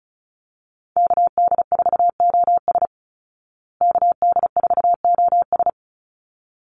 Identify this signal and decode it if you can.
Morse code, heard as a keyed tone, about 34 wpm, 700 Hz. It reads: KD4OS KD4OS
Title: Call Signs Repeated 3x - 35wpm